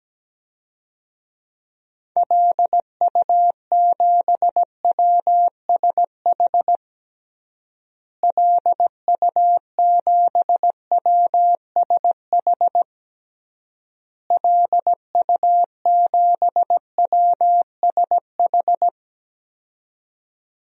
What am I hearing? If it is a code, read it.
Morse code, heard as a keyed tone, about 17 wpm, 700 Hz. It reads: LU7WSH LU7WSH LU7WSH
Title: DXCC-100 Call Signs - 6 Characters - Repeated 3x - 17wpm